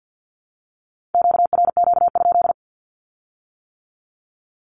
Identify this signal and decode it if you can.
Morse code, heard as a keyed tone, about 50 wpm, 700 Hz. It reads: QRQ?
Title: Q-Codes with Spoken Meaning - 50wpm